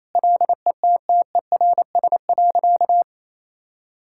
Morse code, heard as a keyed tone, about 28 wpm, 700 Hz. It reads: LETTERS.
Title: Sentences from Top 1000 Words - Review Part 6 - 28wpm